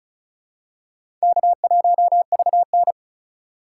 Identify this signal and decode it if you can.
Morse code, heard as a keyed tone, about 36 wpm, 700 Hz. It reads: K1VN
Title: USA Call Signs - 2x Character Spacing - 35wpm